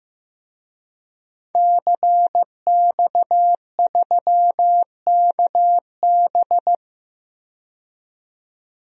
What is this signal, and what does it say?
Morse code, heard as a keyed tone, about 15 wpm, 700 Hz. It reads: CX3KB
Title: DXCC-100 Call Signs - 5 Characters - 15wpm